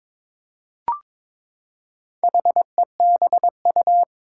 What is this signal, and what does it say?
Morse code, heard as a keyed tone, about 22 wpm, 700 Hz. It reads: HEBU
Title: Four Challenging Letter-Number Combinations 22wpm